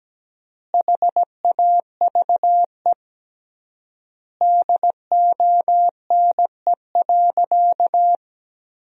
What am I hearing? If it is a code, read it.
Morse code, heard as a keyed tone, about 17 wpm, 700 Hz. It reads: HAVE DONE.
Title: 3x Word-Spacing - Sentences from Top 500 Words - 17wpm